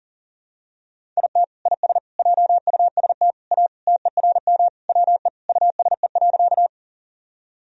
Morse code, heard as a keyed tone, about 37 wpm, 700 Hz. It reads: IT IS JUST A TERM WE USE.
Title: Sentences from Top 1000 Words - Review Part 4 - 40wpm